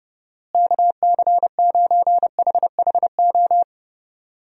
Morse code, heard as a keyed tone, about 30 wpm, 700 Hz. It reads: KC9HHO
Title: USA Call Signs - 6 Characters - Repeated 2x - 30wpm